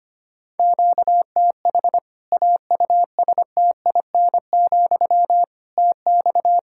Morse code, heard as a keyed tone, about 25 wpm, 700 Hz. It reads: QTH AUSTIN, TX
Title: Practice QSO - Exchange 1 - Beginning - 25wpm